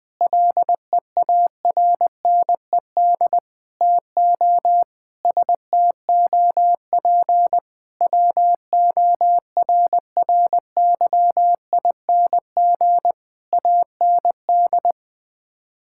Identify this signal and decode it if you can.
Morse code, heard as a keyed tone, about 20 wpm, 700 Hz. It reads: LEARNED TO STOP WORRYING AND
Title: Sets of 8 Words - 20wpm